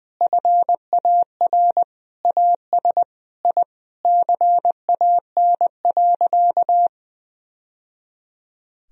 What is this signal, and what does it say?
Morse code, heard as a keyed tone, about 20 wpm, 700 Hz. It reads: FAR AS I CAN.
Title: Sentences from Top 1000 Words - Review Part 1 - 20wpm